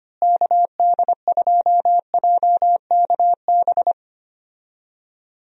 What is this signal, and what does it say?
Morse code, heard as a keyed tone, about 25 wpm, 700 Hz. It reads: KD2JKB
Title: USA Call Signs - 6 Characters - Repeated 2x - 25wpm